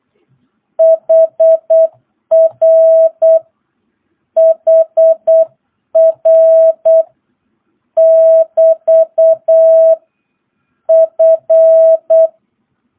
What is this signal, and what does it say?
Morse code, heard as a keyed tone, about 8 wpm, 650 Hz. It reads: HR HR <BT> F
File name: jh8cbh / 2022cw